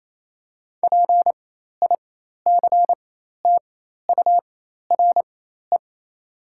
Morse code, heard as a keyed tone, about 28 wpm, spaced as for 13 wpm, 700 Hz. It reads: PICTURE
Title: 4x Character Spacing - Top 300 Words - 28wpm